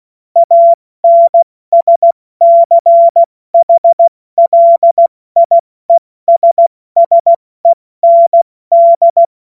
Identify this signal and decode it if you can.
Morse code, heard as a keyed tone, about 16 wpm, 680 Hz. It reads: ANSCHLIESSEND
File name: Witz-16-September-2025-70-F80